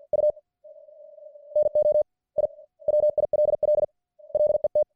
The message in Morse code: U NO I WILL LET